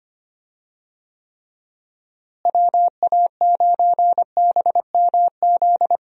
Morse code, heard as a keyed tone, about 25 wpm, 700 Hz. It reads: WA9BMZ
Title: USA Call Signs - 6 Characters - Repeated 2x - 25wpm